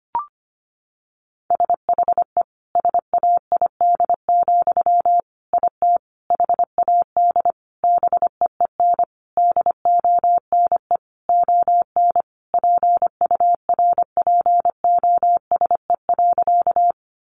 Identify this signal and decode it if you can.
Morse code, heard as a keyed tone, about 25 wpm, 700 Hz. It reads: SHE SAID, IT HAD BEEN DONE ON PURPOSE.
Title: Sentences from Top 800 Words 25wpm